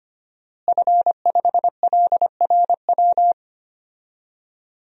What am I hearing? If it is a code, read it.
Morse code, heard as a keyed tone, about 25 wpm, 700 Hz. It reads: F5LRW
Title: DXCC-100 Call Signs - 5 Characters - Repeated 2x - 25wpm